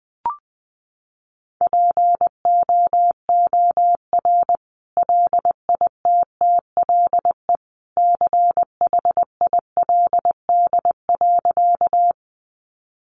Morse code, heard as a keyed tone, about 20 wpm, 700 Hz. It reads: POOR LITTLE CHILD.
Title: Sentences from Top 1000 Words 20wpm